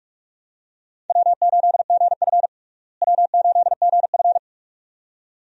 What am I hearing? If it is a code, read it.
Morse code, heard as a keyed tone, about 42 wpm, 700 Hz. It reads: W8GF W8GF
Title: USA Call Signs - 4 Characters - Repeated 2x - 45wpm